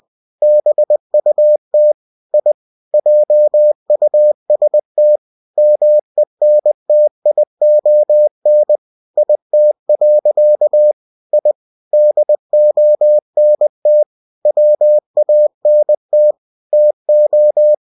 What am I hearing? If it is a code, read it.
Morse code, heard as a keyed tone, about 20 wpm, 600 Hz. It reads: BUT I JUST MENTION IT. I DONT WANT TO
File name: cw-books / winnie-the-pooh / winnie-20wpm-page-0087